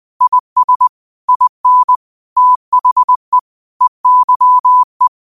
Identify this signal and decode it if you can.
Morse code, heard as a keyed tone, about 20 wpm, 1000 Hz. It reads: IS IN THE EYE
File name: qotdfiles / 20wpm